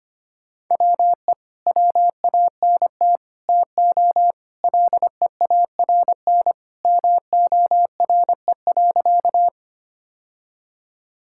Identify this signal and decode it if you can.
Morse code, heard as a keyed tone, about 25 wpm, 700 Hz. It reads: WE WANT TO LEARN MORE.
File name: practice / 25wpm / Sentences with 3-Word Phrases - 25wpm